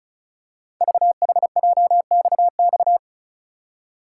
Morse code, heard as a keyed tone, about 36 wpm, 690 Hz. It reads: VHJXX
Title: Code Groups - 5 Letters - 35wpm